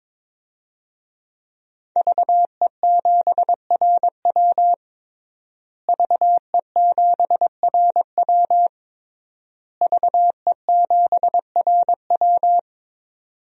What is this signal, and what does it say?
Morse code, heard as a keyed tone, about 22 wpm, 700 Hz. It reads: VE7RW VE7RW VE7RW
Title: DXCC-100 Call Signs - 5 Characters - Repeated 3x - 22wpm